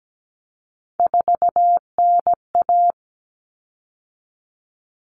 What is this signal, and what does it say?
Morse code, heard as a keyed tone, about 17 wpm, 700 Hz. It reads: 4NA
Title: Common Cut Numbers - 3-Digit 17wpm